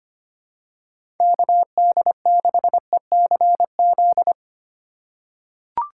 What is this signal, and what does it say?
Morse code, heard as a keyed tone, about 25 wpm, 700 Hz. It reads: KD6ECZ
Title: USA Call Signs - 6 Characters - 3x Character Spacing - 25wpm